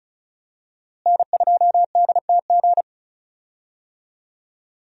Morse code, heard as a keyed tone, about 34 wpm, 700 Hz. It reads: N2DTG
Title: USA Call Signs - 5 Characters - Repeated 3x - 35wpm